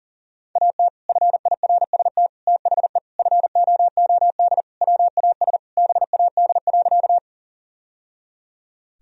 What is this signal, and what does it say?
Morse code, heard as a keyed tone, about 40 wpm, 700 Hz. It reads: AT FIRST THE FOOD WAS BAD.
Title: Sentences from Top 1000 Words - Review Part 1 - 40wpm